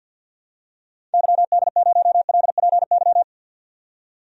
Morse code, heard as a keyed tone, about 48 wpm, 700 Hz. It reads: YD0LPY
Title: DXCC-100 Call Signs - 6 Characters - 50wpm